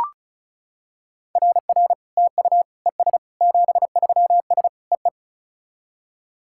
Morse code, heard as a keyed tone, about 34 wpm, 690 Hz. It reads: RR TU ES 73S EE
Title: SOTA QSO Practice 35wpm